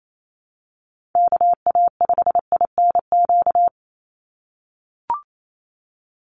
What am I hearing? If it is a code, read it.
Morse code, heard as a keyed tone, about 28 wpm, 700 Hz. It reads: KA5INQ
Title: USA Call Signs 28wpm